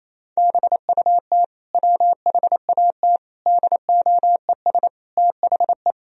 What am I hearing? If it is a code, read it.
Morse code, heard as a keyed tone, about 28 wpm, 700 Hz. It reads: BUT WHAT DOES THE
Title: Sentences from Top 1000 Words - Review Part 2 - 28wpm